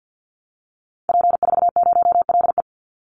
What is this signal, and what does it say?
Morse code, heard as a keyed tone, about 50 wpm, 700 Hz. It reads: P40LE
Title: Call Signs - 50wpm